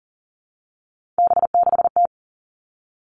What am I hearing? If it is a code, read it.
Morse code, heard as a keyed tone, about 38 wpm, 700 Hz. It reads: B6T
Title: Three Challenging Letter-Number Combinations 40wpm